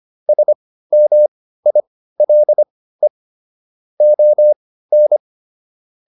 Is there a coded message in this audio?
Yes: Morse code from 0.3 to 5.2 s, heard as a keyed tone, about 25 wpm, spaced as for 15 wpm, 600 Hz. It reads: SMILE ON